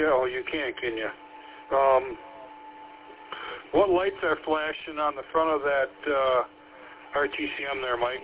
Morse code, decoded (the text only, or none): none